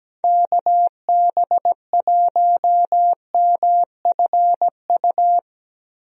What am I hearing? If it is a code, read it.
Morse code, heard as a keyed tone, about 17 wpm, 700 Hz. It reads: KB1MFU
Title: USA Call Signs - 6 Characters - 17wpm